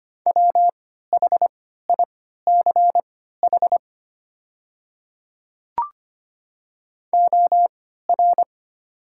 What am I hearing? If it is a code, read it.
Morse code, heard as a keyed tone, about 25 wpm, spaced as for 14 wpm, 690 Hz. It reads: WHICH OR
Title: 3x Character Spacing - Top 200 Words - 25wpm